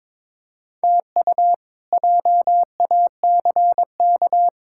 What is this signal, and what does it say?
Morse code, heard as a keyed tone, about 21 wpm, 700 Hz. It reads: TU JACK
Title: K1USN SST QSO Practice - 22wpm